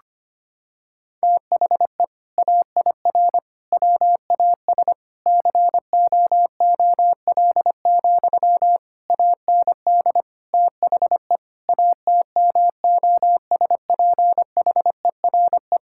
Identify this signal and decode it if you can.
Morse code, heard as a keyed tone, about 25 wpm, 700 Hz. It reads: THE AIR WAS COOL, AND THE ATMOSPHERE
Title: Sentences from Top 1600 Words 25wpm